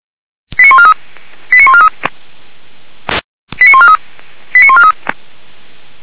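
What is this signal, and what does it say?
Five-tone selective call (ZVEI), heard as quick runs of five tones, tones of about 70 ms. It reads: 89143 89143 89143 89143
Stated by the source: ZVEI five-tone call 89484 received by SDR, logged by Feuerwehr Orlamünde.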